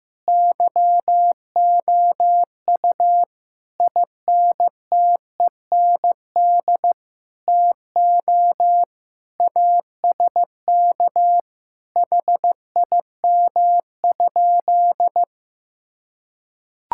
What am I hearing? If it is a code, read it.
Morse code, heard as a keyed tone, about 15 wpm, 700 Hz. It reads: YOU INTEND TO ASK HIM?
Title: Sentences from Top 1600 Words 15wpm